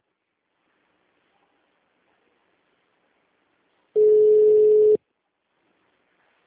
Ringback tone (Europe, 425 Hz).